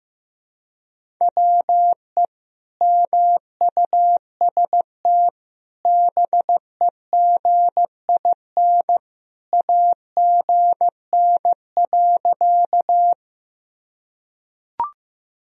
Morse code, heard as a keyed tone, about 15 wpm, 700 Hz. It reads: WE MUST BEGIN AGN.
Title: Sentences from Top 300 Words and Abbreviations 15wpm